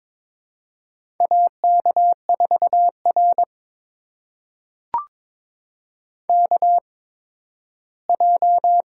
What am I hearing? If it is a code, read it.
Morse code, heard as a keyed tone, about 23 wpm, spaced as for 13 wpm, 700 Hz. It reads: AK4R KJ
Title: USA Call Signs - 8x Character Spacing - 22wpm